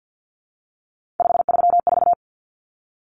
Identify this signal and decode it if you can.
Morse code, heard as a keyed tone, about 50 wpm, 700 Hz. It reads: 534
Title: Common Cut Numbers - 3-Digit 50wpm